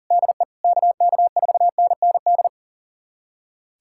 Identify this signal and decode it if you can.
Morse code, heard as a keyed tone, about 40 wpm, 700 Hz. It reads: DE KK4NND
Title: USA Callsign DE Callsign 40wpm